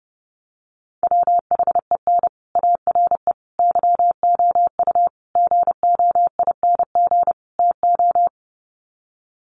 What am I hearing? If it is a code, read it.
Morse code, heard as a keyed tone, about 30 wpm, 700 Hz. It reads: WHEN ARE YOU GOING TO
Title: Sets of 5 Words Encore2 30wpm